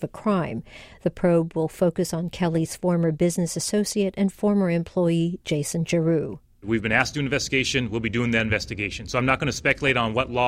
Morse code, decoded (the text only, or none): none